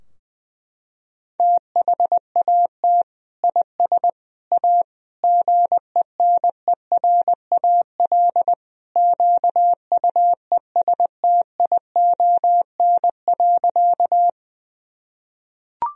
Morse code, heard as a keyed tone, about 21 wpm, 700 Hz. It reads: THAT IS A GENERAL QUESTION.
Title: Sentences from Top 1000 Words - Review Part 3 - 20wpm